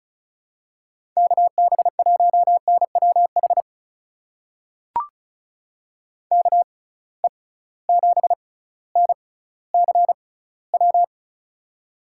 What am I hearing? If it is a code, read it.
Morse code, heard as a keyed tone, about 36 wpm, spaced as for 18 wpm, 700 Hz. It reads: KB1NWH KE7NCW
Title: USA Call Signs - 6 Characters - 6x Character Spacing - 35wpm